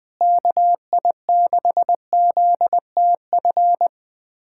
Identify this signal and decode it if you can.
Morse code, heard as a keyed tone, about 20 wpm, 700 Hz. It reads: KI6ZTF
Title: USA Callsign DE Callsign - 6 Characters - 20wpm